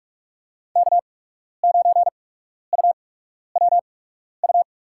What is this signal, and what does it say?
Morse code, heard as a keyed tone, about 46 wpm, spaced as for 12 wpm, 700 Hz. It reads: K9UWU
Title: USA Call Signs - 5 Characters - 8x Character Spacing - 45wpm